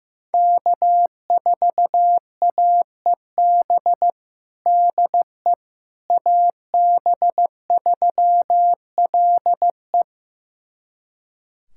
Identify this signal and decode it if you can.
Morse code, heard as a keyed tone, about 15 wpm, 700 Hz. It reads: K4AEB DE AB3LE
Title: USA Callsign DE Callsign - 5 Characters - 15wpm